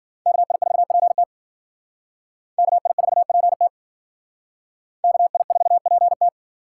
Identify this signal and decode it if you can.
Morse code, heard as a keyed tone, about 51 wpm, 700 Hz. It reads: KI4PT KI4PT KI4PT